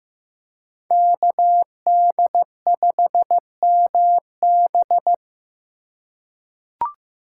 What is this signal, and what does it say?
Morse code, heard as a keyed tone, about 15 wpm, 700 Hz. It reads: KD5MB
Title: USA Call Signs - 5 Characters - 4x Character Spacing - 15wpm